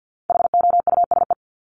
Morse code, heard as a keyed tone, about 50 wpm, 700 Hz. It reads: HOUSE
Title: Top 500 Words Repeated 50wpm